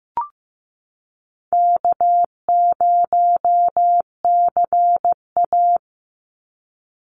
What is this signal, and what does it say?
Morse code, heard as a keyed tone, about 15 wpm, 700 Hz. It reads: K0CA
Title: Call Signs Repeated 2x - 15wpm